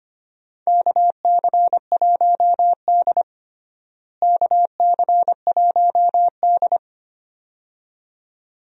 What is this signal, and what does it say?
Morse code, heard as a keyed tone, about 25 wpm, 700 Hz. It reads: KC1D KC1D
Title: USA Call Signs - 4 Characters - Repeated 2x - 25wpm